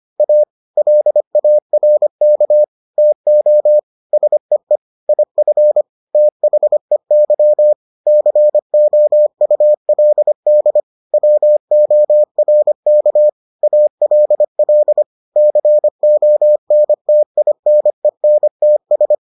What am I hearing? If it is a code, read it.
Morse code, heard as a keyed tone, about 25 wpm, 600 Hz. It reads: A LARK TO SEE IF THEY COULD WORK ALL CONTINENTS